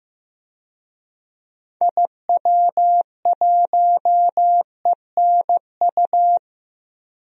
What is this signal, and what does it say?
Morse code, heard as a keyed tone, about 16 wpm, 700 Hz. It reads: IW1ENU
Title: DXCC-100 Call Signs - 6 Characters - 15wpm